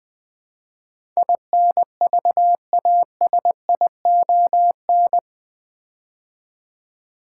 Morse code, heard as a keyed tone, about 20 wpm, 700 Hz. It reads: INVASION
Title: Build A Word - 4 Letters At A Time - 8-Letter Words - 20wpm